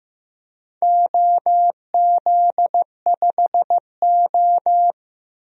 Morse code, heard as a keyed tone, about 15 wpm, 700 Hz. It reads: OZ5O
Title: DXCC-100 Call Signs - 4 Characters - 15wpm